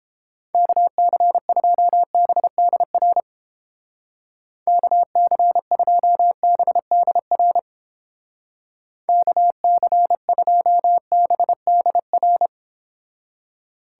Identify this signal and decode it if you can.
Morse code, heard as a keyed tone, about 31 wpm, 700 Hz. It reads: KC2BDR KC2BDR KC2BDR